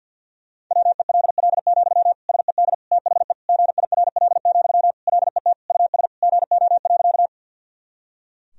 Morse code, heard as a keyed tone, about 52 wpm, 700 Hz. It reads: WELL, SED THE GIRL, LET US GO.
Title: Sentences from Top 400 Words and Abbreviations 50wpm